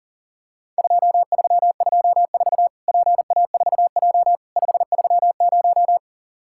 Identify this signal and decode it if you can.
Morse code, heard as a keyed tone, about 40 wpm, 700 Hz. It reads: 2324 PA4J 530
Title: Worked All Europe DX Contest - QTC Exchange - 40wpm